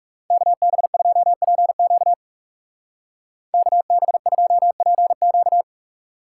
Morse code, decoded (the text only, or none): KB2PQ KB2PQ